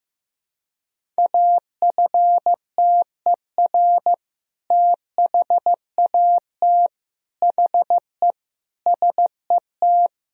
Morse code, heard as a keyed tone, about 16 wpm, 700 Hz. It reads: AFTER THAT HE SET
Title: Sentences from Top 1000 Words - Review Part 1 - 15wpm